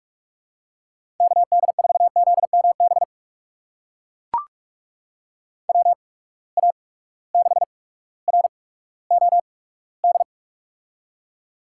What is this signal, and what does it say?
Morse code, heard as a keyed tone, about 44 wpm, spaced as for 19 wpm, 700 Hz. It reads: KD4ZMB WA6ROD
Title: USA Call Signs - 6 Characters - 8x Character Spacing - 45wpm